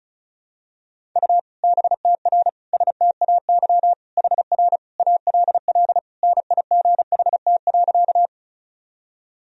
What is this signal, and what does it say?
Morse code, heard as a keyed tone, about 34 wpm, 700 Hz. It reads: U BTR STAY HR ALL NIGHT.
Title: Sentences from Top 500 Words and Abbreviations 35wpm